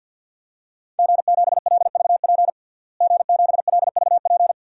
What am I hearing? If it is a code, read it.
Morse code, heard as a keyed tone, about 46 wpm, 700 Hz. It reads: G7LVP G7LVP
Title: DXCC-100 Call Signs - 5 Characters - Repeated 2x - 50wpm